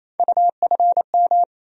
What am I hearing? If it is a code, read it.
Morse code, heard as a keyed tone, about 28 wpm, 700 Hz. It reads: UFM
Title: USA Call Sign Postfixes - 3 Letters 28wpm